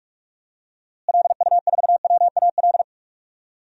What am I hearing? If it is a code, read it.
Morse code, heard as a keyed tone, about 46 wpm, 700 Hz. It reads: LU4WAL
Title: DXCC-100 Call Signs - 6 Characters - Repeated 3x - 45wpm